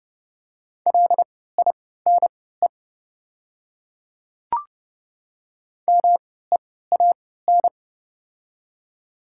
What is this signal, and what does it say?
Morse code, heard as a keyed tone, about 30 wpm, spaced as for 17 wpm, 700 Hz. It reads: LINE MEAN
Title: 3x Character Spacing - Top 200 Words - 30wpm